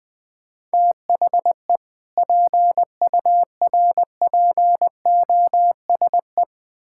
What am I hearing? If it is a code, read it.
Morse code, heard as a keyed tone, about 21 wpm, 700 Hz. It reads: THE PURPOSE
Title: Build A Phrase - 5-Word Phrases - 20wpm